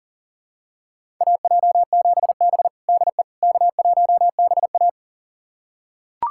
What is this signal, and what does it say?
Morse code, heard as a keyed tone, about 37 wpm, 700 Hz. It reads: AJ7B DE K1BA
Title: USA Callsign DE Callsign - 4 Characters - 40wpm